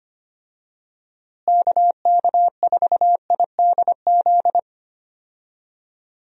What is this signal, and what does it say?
Morse code, heard as a keyed tone, about 25 wpm, 700 Hz. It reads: KK4IDZ
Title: USA Call Signs - 6 Characters - Repeated 2x - 25wpm